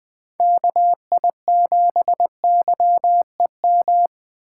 Morse code, heard as a keyed tone, about 20 wpm, 700 Hz. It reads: KI7YEM
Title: USA Call Signs - 6 Characters - 3x Character Spacing - 20wpm